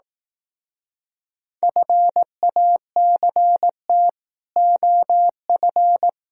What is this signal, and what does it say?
Morse code, heard as a keyed tone, about 18 wpm, 700 Hz. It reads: FACT OF
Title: Speed-Racing - Sets of 5 Words 15wpm